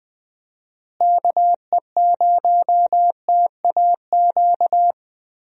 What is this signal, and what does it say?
Morse code, heard as a keyed tone, about 20 wpm, 700 Hz. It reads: KE0TAQ
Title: USA Call Signs - 6 Characters - Repeated 2x - 20wpm